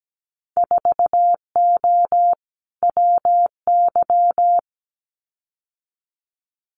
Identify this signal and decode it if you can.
Morse code, heard as a keyed tone, about 17 wpm, 700 Hz. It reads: 4O WY
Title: Winter Field Day Exchanges Encore 17wpm